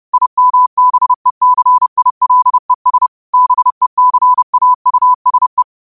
Morse code, heard as a keyed tone, about 30 wpm, 1000 Hz. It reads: IMBECILES BECAUSE
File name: qotdfiles / 30wpm